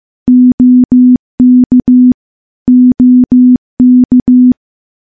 Morse code, heard as a keyed tone, about 15 wpm, 250 Hz. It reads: OK OK
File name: morse / morse-training-15WPM